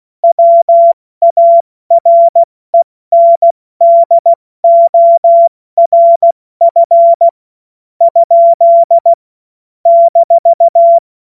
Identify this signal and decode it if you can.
Morse code, heard as a keyed tone, about 16 wpm, spaced as for 13 wpm, 680 Hz. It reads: WARENDORF? -
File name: Witz-22-Juli-2025-70-F80